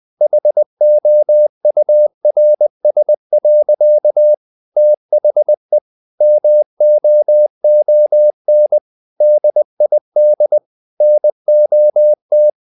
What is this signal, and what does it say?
Morse code, heard as a keyed tone, about 20 wpm, 600 Hz. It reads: HOURS. THE MOON DID NOT
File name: the old man and the sea part 2 20wpm